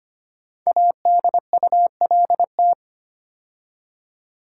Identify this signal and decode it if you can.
Morse code, heard as a keyed tone, about 25 wpm, 700 Hz. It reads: ADULT
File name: 5 Letter Words - First 3 Letters Given - 25wpm